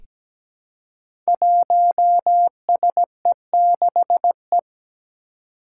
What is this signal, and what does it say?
Morse code, heard as a keyed tone, about 17 wpm, 700 Hz. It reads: 1SE6E
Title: Five Challenging Letter-Number Combinations 17wpm